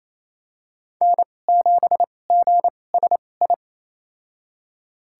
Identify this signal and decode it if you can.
Morse code, heard as a keyed tone, about 28 wpm, spaced as for 20 wpm, 700 Hz. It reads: N7GSI
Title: USA Call Signs - 5 Characters - 2x Character Spacing - 28wpm